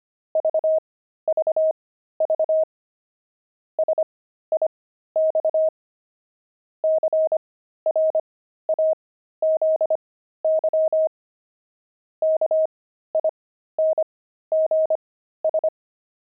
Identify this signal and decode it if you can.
Morse code, heard as a keyed tone, about 25 wpm, spaced as for 13 wpm, 630 Hz. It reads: VVV SIX CRAZY KINGS
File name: pangram-fs-13-cs-25-crazy